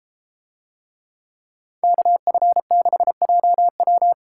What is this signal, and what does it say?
Morse code, heard as a keyed tone, about 31 wpm, 700 Hz. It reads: KF6JW